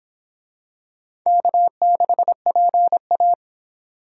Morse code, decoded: K6PA

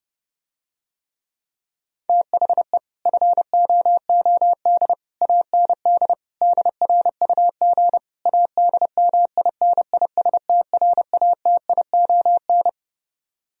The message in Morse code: THE FOOD AND DRUG ADMINISTRATION

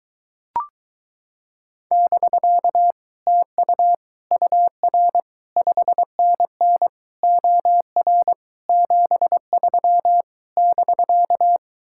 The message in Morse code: <BK> TU UR 5NN OR 73 <BK>

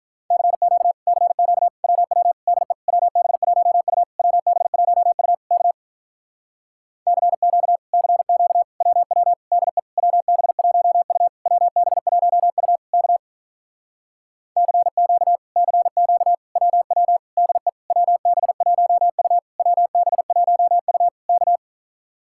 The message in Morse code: CQ CQ WW DE WB1U WB1U K CQ CQ WW DE WB1U WB1U K CQ CQ WW DE WB1U WB1U K

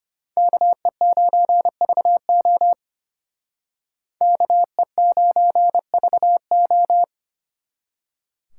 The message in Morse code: KE9VO KE9VO

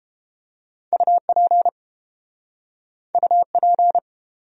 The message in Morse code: UP UP

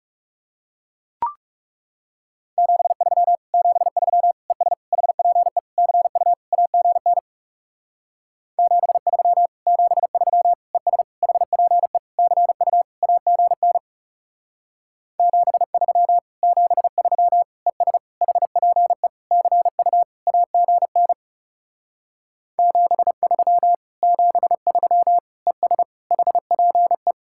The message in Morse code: 73 73 ES HPE CU AGN 73 73 ES HPE CU AGN 73 73 ES HPE CU AGN 73 73 ES HPE

